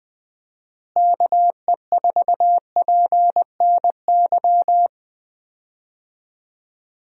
Morse code, heard as a keyed tone, about 20 wpm, 700 Hz. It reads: KE4PNY
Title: USA Call Signs - 6 Characters - Repeated 3x - 20wpm